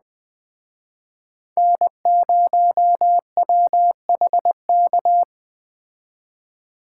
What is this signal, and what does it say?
Morse code, heard as a keyed tone, about 20 wpm, 700 Hz. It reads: N0WHK
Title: Speed Racing - USA 1x3 Call Signs 17wpm